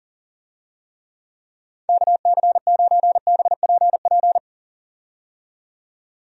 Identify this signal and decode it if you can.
Morse code, heard as a keyed tone, about 37 wpm, 700 Hz. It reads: KC9BPP
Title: USA Call Signs - 6 Characters - 40wpm